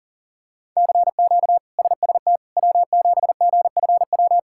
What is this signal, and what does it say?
Morse code, heard as a keyed tone, about 40 wpm, 700 Hz. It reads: CQ SST W7GFW
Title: K1USN SST QSO Practice - 40wpm